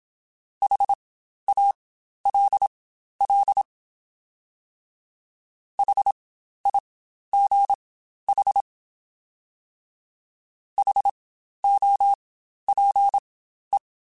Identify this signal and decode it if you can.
Morse code, heard as a keyed tone, about 27 wpm, spaced as for 12 wpm, 800 Hz. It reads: HALL HIGH HOPE
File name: Spaced Alphas